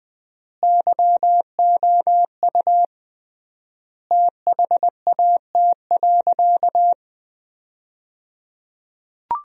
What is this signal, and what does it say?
Morse code, heard as a keyed tone, about 20 wpm, 700 Hz. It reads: YOU THAT.
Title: 3x Word-Spacing - Sentences from Top 300 Words - 20wpm